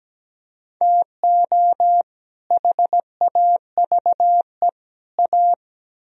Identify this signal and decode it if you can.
Morse code, heard as a keyed tone, about 20 wpm, 700 Hz. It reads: TO HAVE A
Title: Speed-Racing - Sets of 3 Words 17wpm